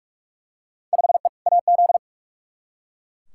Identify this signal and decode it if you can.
Morse code, heard as a keyed tone, about 42 wpm, 700 Hz. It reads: 5E AZ
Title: Field Day Exchanges - 45wpm